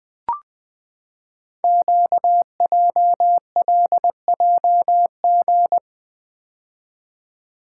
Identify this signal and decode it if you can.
Morse code, heard as a keyed tone, about 20 wpm, 700 Hz. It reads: QJLJG
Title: Code Groups - 5 Letters - 20wpm